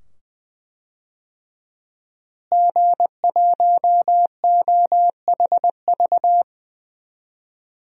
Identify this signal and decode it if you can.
Morse code, heard as a keyed tone, about 20 wpm, 700 Hz. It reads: G1OHV
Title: DXCC-100 Call Signs - 5 Characters - Repeated 2x - 20wpm